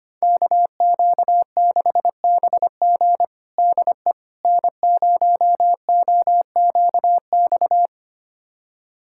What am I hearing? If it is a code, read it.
Morse code, heard as a keyed tone, about 25 wpm, 700 Hz. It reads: KQ6BG DE N0OQX
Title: USA Callsign DE Callsign - 5 Characters - 25wpm